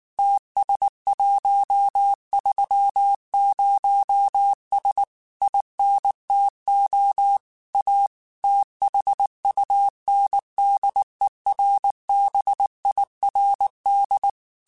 Morse code, heard as a keyed tone, about 19 wpm, 800 Hz. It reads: TS130S INTO A THUNDERBIRD
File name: QSOs / QSOs-mp3s / WD0BGP 1@18